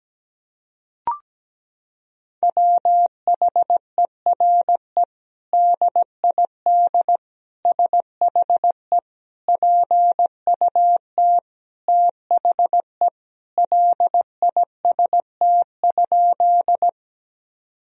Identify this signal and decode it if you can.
Morse code, heard as a keyed tone, about 17 wpm, 700 Hz. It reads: WHERE DID SHE PUT THE LIST?
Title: Sentences from Top 1000 Words - Review Part 2 - 17wpm